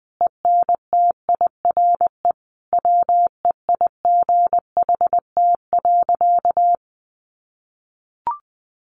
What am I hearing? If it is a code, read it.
Morse code, heard as a keyed tone, about 20 wpm, 700 Hz. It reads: ENTIRE WEIGHT.
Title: Sentences from Top 800 Words 20wpm